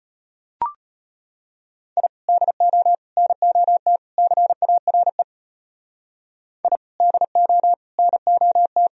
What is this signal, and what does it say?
Morse code, heard as a keyed tone, about 32 wpm, 700 Hz. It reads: I DO NOT CARE I DO NOT